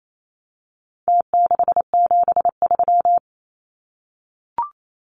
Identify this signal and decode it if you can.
Morse code, heard as a keyed tone, about 28 wpm, 700 Hz. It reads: T673
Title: Numbers - Contest Style - 4-Digit 28wpm